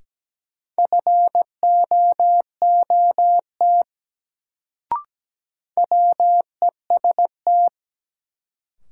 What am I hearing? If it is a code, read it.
Morse code, heard as a keyed tone, about 17 wpm, 700 Hz. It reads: FOOT WEST
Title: Sets of 5 Words - New Word Drill - 17wpm